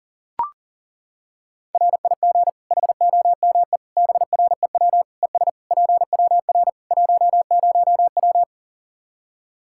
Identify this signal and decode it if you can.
Morse code, heard as a keyed tone, about 40 wpm, 700 Hz. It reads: RIG HOME BREW ES PWR 10W
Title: Practice QSO - Exchange 2 - Info - 40wpm